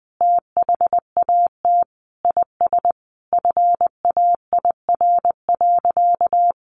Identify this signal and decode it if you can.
Morse code, heard as a keyed tone, about 20 wpm, 700 Hz. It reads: THAT IS FAIR.
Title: Sentences from Top 1300 Words 20wpm